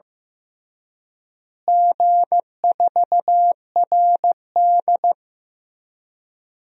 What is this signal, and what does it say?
Morse code, heard as a keyed tone, about 15 wpm, 700 Hz. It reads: G4RD
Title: DXCC-100 Call Signs - 4 Characters - Repeated 3x - 15wpm